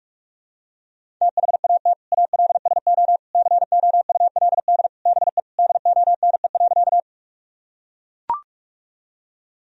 Morse code, heard as a keyed tone, about 41 wpm, 700 Hz. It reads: THAT ALSO COULD BE DONE.
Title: Sentences from Top 1000 Words - Review Part 1 - 45wpm